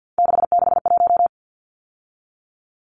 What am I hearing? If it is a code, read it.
Morse code, heard as a keyed tone, about 50 wpm, 700 Hz. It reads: 661
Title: Numbers - 3-Digit 50wpm